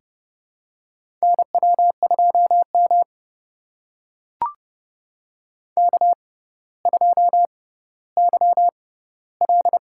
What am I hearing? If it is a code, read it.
Morse code, heard as a keyed tone, about 30 wpm, spaced as for 15 wpm, 700 Hz. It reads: NW2M K2YL
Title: USA Call Signs - 4 Characters - 6x Character Spacing - 30wpm